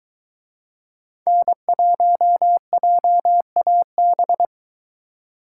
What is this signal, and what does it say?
Morse code, heard as a keyed tone, about 23 wpm, 700 Hz. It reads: N1JAB